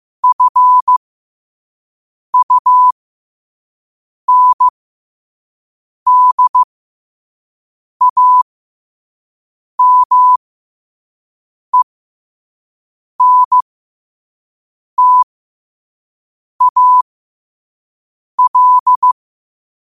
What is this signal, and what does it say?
Morse code, heard as a keyed tone, about 15 wpm, spaced as for 5 wpm, 1000 Hz. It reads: FUNDAMENTAL
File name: qotdfiles / 5wpm